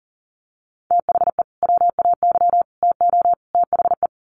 Morse code, heard as a keyed tone, about 40 wpm, 700 Hz. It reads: THE WAY TO THE
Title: Sets of 4 Words 40wpm